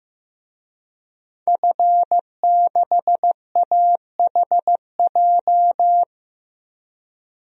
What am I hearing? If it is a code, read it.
Morse code, heard as a keyed tone, about 15 wpm, 700 Hz. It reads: F6AHJ
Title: DXCC-100 Call Signs - 5 Characters - 15wpm